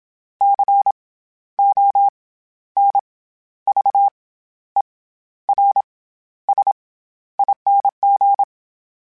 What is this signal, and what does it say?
Morse code, heard as a keyed tone, about 27 wpm, 800 Hz. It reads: C O N V E R S ING
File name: Special INGs-mp3s / Special INGs-mp3s / SlowToFaster ing @25